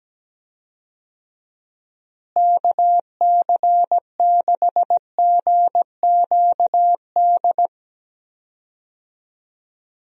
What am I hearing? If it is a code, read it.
Morse code, heard as a keyed tone, about 17 wpm, 700 Hz. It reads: KC6GQD
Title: USA Call Signs - 6 Characters - Repeated 2x - 17wpm